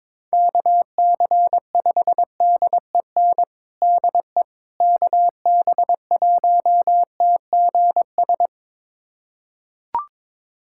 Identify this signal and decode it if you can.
Morse code, heard as a keyed tone, about 21 wpm, 700 Hz. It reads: KC5DEN DE KB1TGS
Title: USA Callsign DE Callsign - 6 Characters - 22wpm